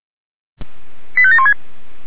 Five-tone selective call (ZVEI), heard as quick runs of five tones, tones of about 70 ms. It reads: 85716
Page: SDR/ZVEI